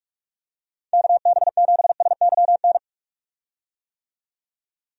Morse code, heard as a keyed tone, about 46 wpm, 700 Hz. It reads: KB7SYN
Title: USA Call Signs - 6 Characters - 45wpm